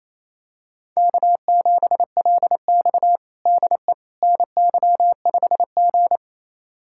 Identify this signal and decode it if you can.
Morse code, heard as a keyed tone, about 28 wpm, 700 Hz. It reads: K7LX DE NY5G
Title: USA Callsign DE Callsign - 4 Characters - 28wpm